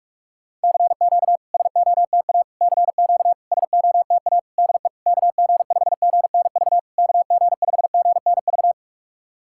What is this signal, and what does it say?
Morse code, heard as a keyed tone, about 47 wpm, 700 Hz. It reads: CQ SOTA CQ SOTA DE KG5GNV KG5GNV
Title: SOTA QSO Practice 45wpm